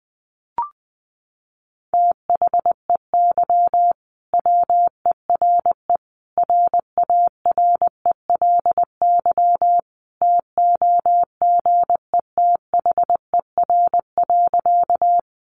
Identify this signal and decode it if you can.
Morse code, heard as a keyed tone, about 20 wpm, 700 Hz. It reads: THEY WERE RARELY TOGETHER.
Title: Sentences from Top 1400 Words 20wpm